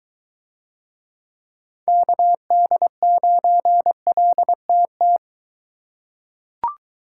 Morse code, heard as a keyed tone, about 21 wpm, 700 Hz. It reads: KD9LTT